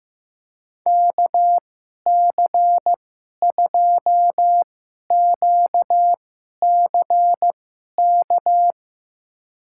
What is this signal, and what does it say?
Morse code, heard as a keyed tone, about 15 wpm, spaced as for 11 wpm, 700 Hz. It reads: KC2QCK
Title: USA Call Signs - 2x Character Spacing - 15wpm